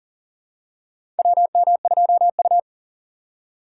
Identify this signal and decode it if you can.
Morse code, heard as a keyed tone, about 40 wpm, 700 Hz. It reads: WM2U
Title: USA Call Signs - 4 Characters - 4x Character Spacing - 40wpm